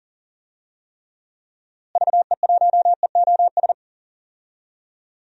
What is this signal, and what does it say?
Morse code, heard as a keyed tone, about 40 wpm, 700 Hz. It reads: VE1EOS
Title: DXCC-100 Call Signs - 6 Characters - Repeated 3x - 40wpm